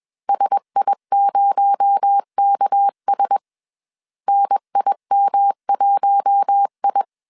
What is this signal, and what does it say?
Morse code, heard as a keyed tone, about 21 wpm, 800 Hz. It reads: SI0KS NIM1I
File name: CW / DET / 1 6 1500C S A 20wpm 2048